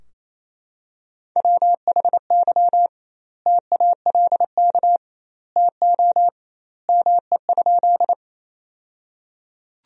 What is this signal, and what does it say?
Morse code, heard as a keyed tone, about 28 wpm, 700 Hz. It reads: WHY TALK TO ME?
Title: 2x Word-Spacing - Sentences from Top 400 Words - 28wpm